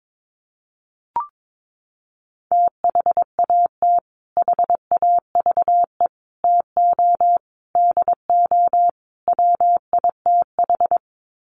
Morse code, heard as a keyed tone, about 22 wpm, 700 Hz. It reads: THAT HAVE TO DO WITH
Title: Sets of 5 Words Encore2 22wpm